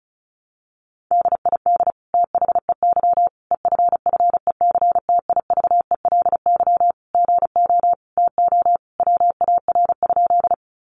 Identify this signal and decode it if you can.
Morse code, heard as a keyed tone, about 34 wpm, 700 Hz. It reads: DID THEY EFFECTIVELY GO TO WAR?
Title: Sentences from Top 1400 Words 35wpm